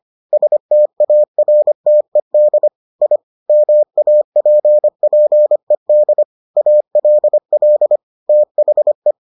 Morse code, heard as a keyed tone, about 25 wpm, 600 Hz. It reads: STARTED I MAPPED ALL THE